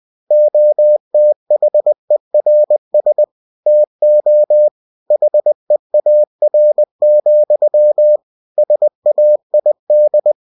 Morse code, heard as a keyed tone, about 20 wpm, 600 Hz. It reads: OTHERS TO HEAR, SAID
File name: cw-books / winnie-the-pooh / winnie-20wpm-page-0084